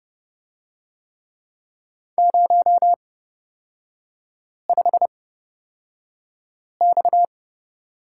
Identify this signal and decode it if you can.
Morse code, heard as a keyed tone, about 30 wpm, spaced as for 5 wpm, 700 Hz. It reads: 05X
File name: Lesson 37e - Letter x - Three Characters 30wpm